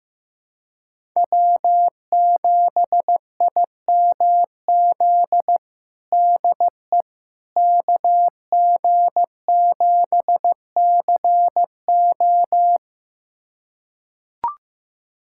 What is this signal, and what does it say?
Morse code, heard as a keyed tone, about 15 wpm, 700 Hz. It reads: W7IMZ DE KG7CO
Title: USA Callsign DE Callsign - 5 Characters - 15wpm